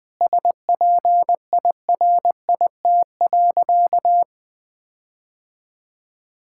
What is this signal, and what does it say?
Morse code, heard as a keyed tone, about 20 wpm, 700 Hz. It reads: SPIRIT.
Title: Sentences with 9-Word Phrases - 20wpm